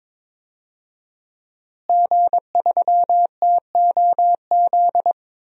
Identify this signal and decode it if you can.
Morse code, heard as a keyed tone, about 21 wpm, 700 Hz. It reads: G3TOZ
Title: DXCC-100 Call Signs - 5 Characters - Repeated 3x - 22wpm